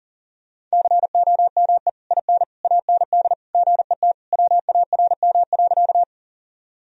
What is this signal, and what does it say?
Morse code, heard as a keyed tone, about 40 wpm, 700 Hz. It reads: COME IN AND GET WARM.
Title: Sentences from Top 1000 Words - Review Part 2 - 40wpm